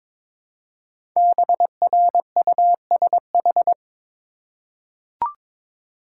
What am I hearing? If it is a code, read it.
Morse code, heard as a keyed tone, about 22 wpm, 700 Hz. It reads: BRUSH
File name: 5 Letter Words - First 3 Letters Given - 22wpm